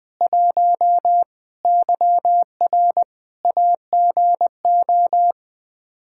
Morse code, heard as a keyed tone, about 20 wpm, 700 Hz. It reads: 1 YR AGO
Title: Practice QSO - Exchange 3 - More Info - 20wpm